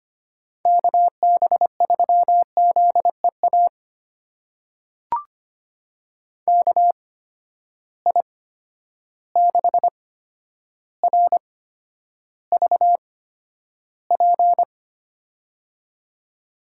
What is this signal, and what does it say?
Morse code, heard as a keyed tone, about 25 wpm, spaced as for 11 wpm, 700 Hz. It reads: KB3ZEA KI6RVP
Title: USA Call Signs - 6 Characters - 8x Character Spacing - 25wpm